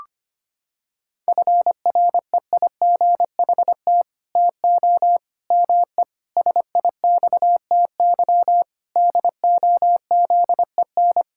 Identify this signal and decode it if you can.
Morse code, heard as a keyed tone, about 25 wpm, 700 Hz. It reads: FREIGHT TO ME SIXTY DOZEN
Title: Pangram Challenge v2 - 25wpm